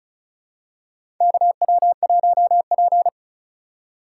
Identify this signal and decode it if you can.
Morse code, heard as a keyed tone, about 34 wpm, 700 Hz. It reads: KW1P
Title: USA Call Signs - 4 Characters - 3x Character Spacing - 35wpm